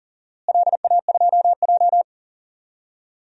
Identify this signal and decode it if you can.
Morse code, heard as a keyed tone, about 40 wpm, 700 Hz. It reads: LA2J
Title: DXCC-100 Call Signs - 4 Characters - Repeated 3x - 40wpm